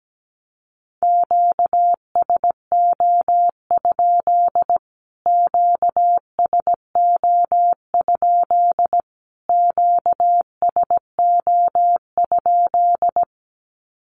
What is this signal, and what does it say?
Morse code, heard as a keyed tone, about 17 wpm, 700 Hz. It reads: QSO? QSO? QSO?
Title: Q-Codes Repeated 3x 17wpm